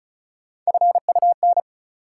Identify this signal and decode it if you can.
Morse code, heard as a keyed tone, about 36 wpm, 700 Hz. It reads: FUN
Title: Sentences from Top 700 Words - New Word Drill - 35wpm